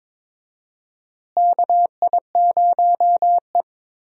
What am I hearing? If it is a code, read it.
Morse code, heard as a keyed tone, about 21 wpm, 700 Hz. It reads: KI0E
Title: USA Call Signs - 4 Characters - Repeated 3x - 22wpm